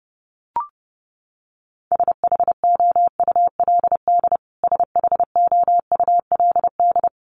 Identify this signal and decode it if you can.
Morse code, heard as a keyed tone, about 30 wpm, 700 Hz. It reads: SHOULD SHOULD
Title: Top 300 Words Repeated 30wpm